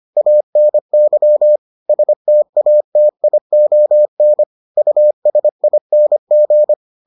Morse code, heard as a keyed tone, about 25 wpm, 600 Hz. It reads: ANY STATION USING